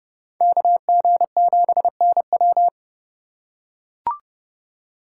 Morse code, heard as a keyed tone, about 30 wpm, 700 Hz. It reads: KG7NW
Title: USA Call Signs - 2x Character Spacing - 30wpm